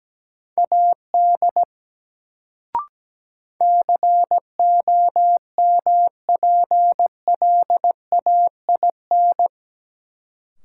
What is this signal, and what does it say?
Morse code, heard as a keyed tone, about 17 wpm, 700 Hz. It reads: AD COMPLAIN
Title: Sentences from Top 1300 Words - New Word Drill - 17wpm